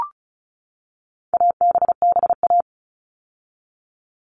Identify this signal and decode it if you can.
Morse code, heard as a keyed tone, about 34 wpm, 700 Hz. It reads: ABBA
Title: All Cut Numbers - 4-Digit 35wpm